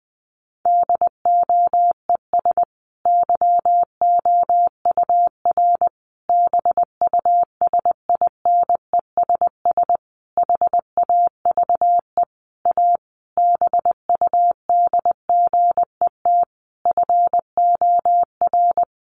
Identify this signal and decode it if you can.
Morse code, heard as a keyed tone, about 20 wpm, 700 Hz. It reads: DOES YOUR BUSINESS HAVE A BUDGET FOR
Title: Sentences from Top 1200 Words 20wpm